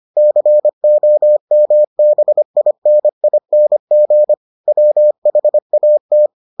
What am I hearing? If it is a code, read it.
Morse code, heard as a keyed tone, about 25 wpm, 600 Hz. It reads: COMBINING WHAT